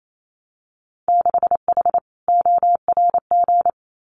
Morse code, heard as a keyed tone, about 28 wpm, 700 Hz. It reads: 6H ORG
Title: Winter Field Day Exchanges Encore 28wpm